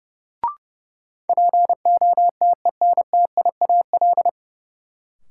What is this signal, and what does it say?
Morse code, heard as a keyed tone, about 30 wpm, 700 Hz. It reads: POTENTIAL
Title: Sentences from Top 1000 Words - Review Part 4 - New Word Drill - 30wpm